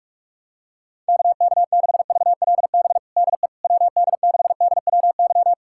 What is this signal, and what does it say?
Morse code, heard as a keyed tone, about 41 wpm, 700 Hz. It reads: KK6VLB DE WD6DWY
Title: USA Callsign DE Callsign - 6 Characters - 45wpm